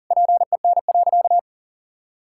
Morse code, heard as a keyed tone, about 40 wpm, 700 Hz. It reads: PEN.
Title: Pangram Challenge v2 - 40wpm